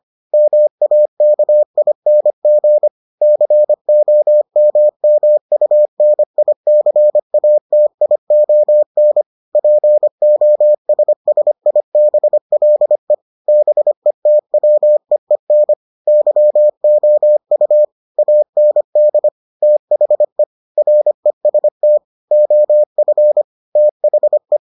Morse code, heard as a keyed tone, about 25 wpm, 600 Hz. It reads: MAKING COMMUNICATION POSSIBLE BETWEEN YOU AND THE REST OF THE